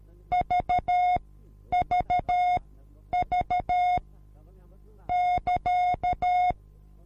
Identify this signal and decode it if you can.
Morse code, heard as a keyed tone, about 13 wpm, spaced as for 9 wpm, 720 Hz. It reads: VVV <KA>